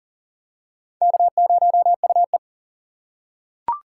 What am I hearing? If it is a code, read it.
Morse code, heard as a keyed tone, about 39 wpm, 690 Hz. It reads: K0UE
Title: USA Call Signs - 4 Characters - 2x Character Spacing - 40wpm